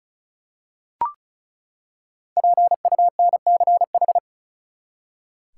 Morse code, heard as a keyed tone, about 34 wpm, 700 Hz. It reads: PUNCH
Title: Five Letter Words - 35wpm